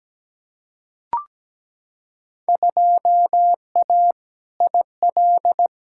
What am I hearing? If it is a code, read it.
Morse code, heard as a keyed tone, about 17 wpm, 700 Hz. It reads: 2A IL
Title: Field Day Exchanges Encore - 17wpm